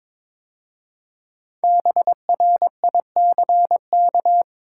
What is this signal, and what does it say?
Morse code, heard as a keyed tone, about 23 wpm, 700 Hz. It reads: BRICK
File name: practice / 22wpm / 5 Letter Words - First 3 Letters Given - 22wpm